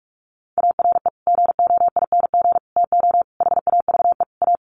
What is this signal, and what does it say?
Morse code, heard as a keyed tone, about 44 wpm, 700 Hz. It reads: ARE GOING TO HAVE A
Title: Sets of 5 Words Encore2 45wpm